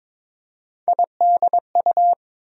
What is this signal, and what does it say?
Morse code, heard as a keyed tone, about 22 wpm, 700 Hz. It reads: IDU
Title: USA Call Sign Postfixes - 3 Letters 22wpm